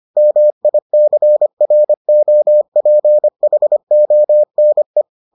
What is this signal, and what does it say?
Morse code, heard as a keyed tone, about 25 wpm, 600 Hz. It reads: MICROPHONE